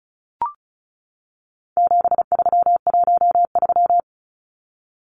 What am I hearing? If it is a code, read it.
Morse code, heard as a keyed tone, about 34 wpm, 700 Hz. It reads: 7313
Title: Numbers - 4-Digit 35wpm